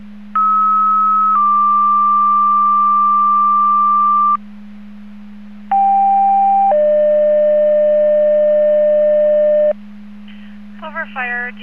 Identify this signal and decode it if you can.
Two-tone page, heard as a one-second tone, then a three-second tone, about 1290 then 1180 Hz, about 790 then 600 Hz. It is heard from 0.3 to 9.7 s.
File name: Portage 154.385 Paging